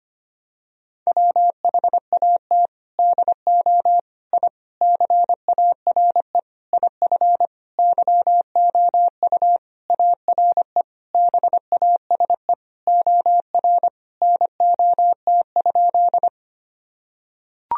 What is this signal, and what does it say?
Morse code, heard as a keyed tone, about 25 wpm, 700 Hz. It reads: WHAT DO I CARE IF YOU ARE BASE OR NOT?
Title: Sentences from Top 1000 Words - Review Part 1 - 25wpm